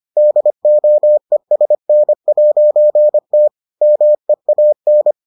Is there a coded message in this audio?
Yes: Morse code from 0.2 to 5.1 s, heard as a keyed tone, about 25 wpm, 600 Hz. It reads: DOESN'T MEAN